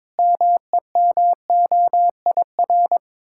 Morse code, heard as a keyed tone, about 22 wpm, 690 Hz. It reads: MEMOIR
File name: Difficult Words with Hints - 22wpm